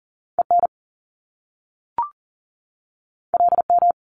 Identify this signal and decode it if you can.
Morse code, heard as a keyed tone, about 41 wpm, 700 Hz. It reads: EN LM
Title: Top Two Letter Combinations in Words 40wpm